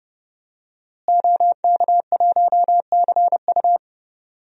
Morse code, heard as a keyed tone, about 30 wpm, 700 Hz. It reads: OK1CU
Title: DXCC-100 Call Signs - 5 Characters - Repeated 2x - 30wpm